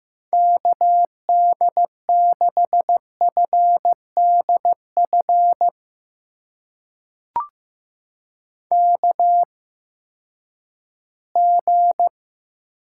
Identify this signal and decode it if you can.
Morse code, heard as a keyed tone, about 15 wpm, spaced as for 10 wpm, 700 Hz. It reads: KD6FDF KG
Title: USA Call Signs - 8x Character Spacing - 15wpm